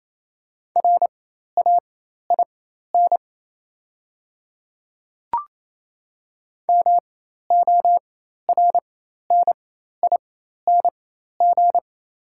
Morse code, heard as a keyed tone, about 27 wpm, spaced as for 13 wpm, 700 Hz. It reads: RAIN MORNING
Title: 4x Character Spacing - Top 500 Words - 28wpm